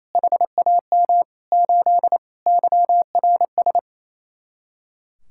Morse code, heard as a keyed tone, about 35 wpm, 700 Hz. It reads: HAM 8 YRS